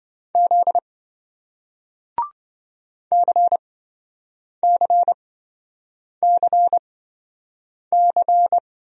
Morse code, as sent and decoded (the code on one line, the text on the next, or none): --.. / -.-. -.-. -.-. -.-.
Z CCCC